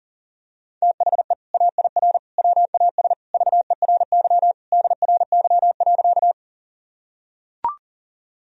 Morse code, heard as a keyed tone, about 40 wpm, 700 Hz. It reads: THE AIR WAS VERY DRY.
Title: Sentences from Top 1000 Words - Review Part 3 - 40wpm